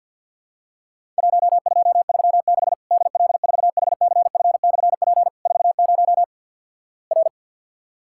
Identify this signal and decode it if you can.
Morse code, heard as a keyed tone, about 50 wpm, 700 Hz. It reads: 1236 DL4HCF/P 40 R
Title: Worked All Europe DX Contest - QTC Exchange - 50wpm